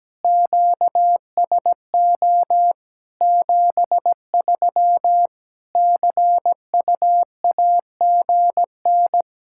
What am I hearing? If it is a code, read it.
Morse code, heard as a keyed tone, about 17 wpm, 700 Hz. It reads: QSO 73 CUAGN
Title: Practice QSO - Exchange 4 - Ending - 17wpm